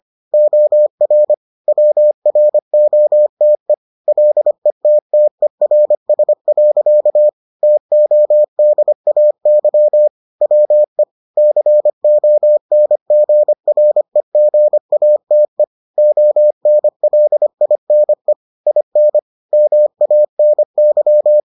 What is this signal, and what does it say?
Morse code, heard as a keyed tone, about 25 wpm, 600 Hz. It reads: OR WROTE LETTERS. TODAY WE CONGREGATE ONLINE IN MANY